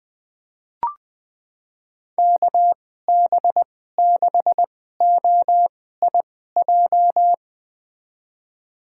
Morse code, heard as a keyed tone, about 20 wpm, spaced as for 14 wpm, 700 Hz. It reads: KB6OIJ
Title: USA Call Signs - 6 Characters - 2x Character Spacing - 20wpm